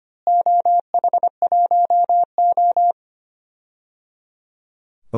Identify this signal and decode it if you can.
Morse code, heard as a keyed tone, about 25 wpm, 700 Hz. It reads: OH1O
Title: DXCC-100 Call Signs - 4 Characters - Repeated 3x - 25wpm